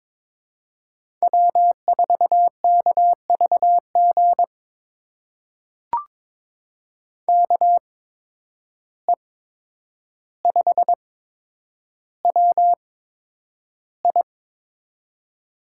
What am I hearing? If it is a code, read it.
Morse code, heard as a keyed tone, about 22 wpm, spaced as for 9 wpm, 700 Hz. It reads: W4KVG KE5WI
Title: USA Call Signs - 5 Characters - 8x Character Spacing - 22wpm